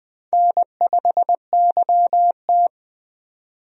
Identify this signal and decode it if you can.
Morse code, heard as a keyed tone, about 21 wpm, 700 Hz. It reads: N5YT
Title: USA Call Signs - 4 Characters - 20wpm